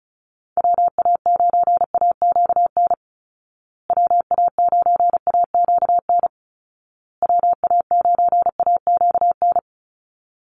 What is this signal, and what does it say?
Morse code, heard as a keyed tone, about 34 wpm, 700 Hz. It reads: WA9AQN WA9AQN WA9AQN
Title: Call Signs Repeated 2x - 35wpm